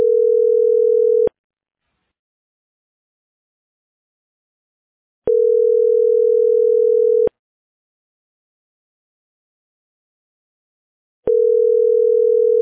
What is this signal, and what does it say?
Ringback tone (North America), 440 + 480 Hz.